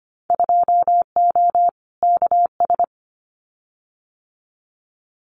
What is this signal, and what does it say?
Morse code, heard as a keyed tone, about 25 wpm, 700 Hz. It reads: 2O KS
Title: Winter Field Day Exchanges Encore 25wpm